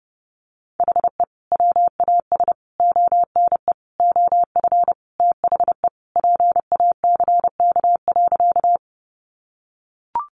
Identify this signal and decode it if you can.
Morse code, heard as a keyed tone, about 30 wpm, 700 Hz. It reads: HE WAS ONE OF THE PACK.
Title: Sentences from Top 1300 Words 30wpm